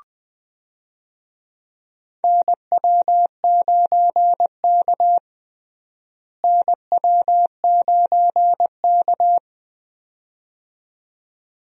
Morse code, heard as a keyed tone, about 20 wpm, 700 Hz. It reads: NW9K NW9K
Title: USA Call Signs - 4 Characters - Repeated 2x - 20wpm